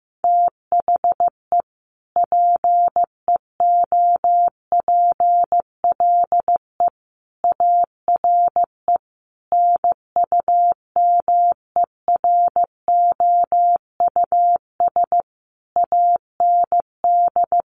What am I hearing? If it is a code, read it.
Morse code, heard as a keyed tone, about 15 wpm, 700 Hz. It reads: THE PEOPLE ARE NUMEROUS AND
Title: Sentences from Top 1400 Words 15wpm